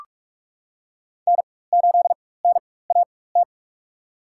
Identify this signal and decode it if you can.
Morse code, heard as a keyed tone, about 46 wpm, spaced as for 21 wpm, 700 Hz. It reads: N8NAT
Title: USA Call Signs - 5 Characters - 4x Character Spacing - 45wpm